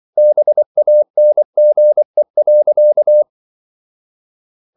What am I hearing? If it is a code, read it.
Morse code, heard as a keyed tone, about 24 wpm, 600 Hz. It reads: BANGE.
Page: Ferie 24 wpm